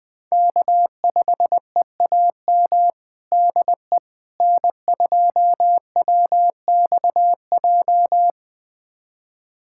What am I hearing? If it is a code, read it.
Morse code, heard as a keyed tone, about 21 wpm, 700 Hz. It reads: K5EAM DE N2WXJ
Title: USA Callsign DE Callsign - 5 Characters - 20wpm